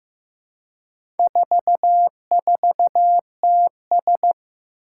Morse code, heard as a keyed tone, about 15 wpm, 700 Hz. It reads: 44TS
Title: Four Challenging Letter-Number Combinations 15wpm